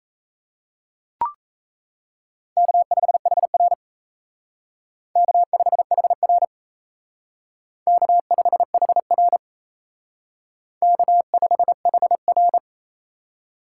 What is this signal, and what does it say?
Morse code, heard as a keyed tone, about 33 wpm, 700 Hz. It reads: K5HR K5HR K5HR K5HR
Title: Speed Racing - USA 1x2 Call Signs 28wpm